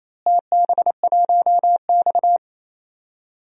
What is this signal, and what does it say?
Morse code, heard as a keyed tone, about 28 wpm, 700 Hz. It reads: TB1X
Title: DXCC-100 Call Signs - 4 Characters - Repeated 2x - 28wpm